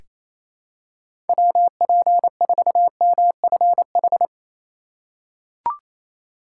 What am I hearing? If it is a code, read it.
Morse code, heard as a keyed tone, about 27 wpm, 700 Hz. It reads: WP4MFH
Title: USA Call Signs - 6 Characters - 3x Character Spacing - 28wpm